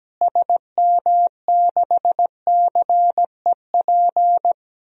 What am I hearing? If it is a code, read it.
Morse code, heard as a keyed tone, about 17 wpm, 700 Hz. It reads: SM6CEP
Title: DXCC-100 Call Signs - 6 Characters - Repeated 3x - 17wpm